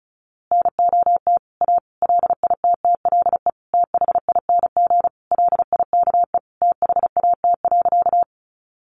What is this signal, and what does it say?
Morse code, heard as a keyed tone, about 34 wpm, 700 Hz. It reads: NOT A LITTLE THING LIKE THAT.
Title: A Sound Of Thunder 35wpm